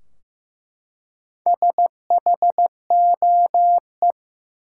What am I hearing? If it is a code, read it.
Morse code, heard as a keyed tone, about 15 wpm, 700 Hz. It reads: SHOE
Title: Four Letter Words - 15wpm